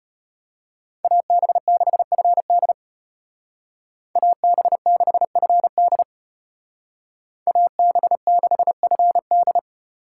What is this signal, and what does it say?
Morse code, heard as a keyed tone, about 33 wpm, 700 Hz. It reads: AB6FD AB6FD AB6FD